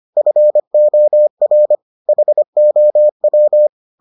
Morse code, heard as a keyed tone, about 25 wpm, 600 Hz. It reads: FOR HOW